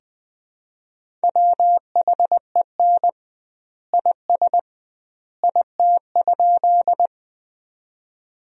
Morse code, heard as a keyed tone, about 20 wpm, 700 Hz. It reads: WHEN IS IT?
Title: 2x Word-Spacing - Sentences from Top 100 Words - 20wpm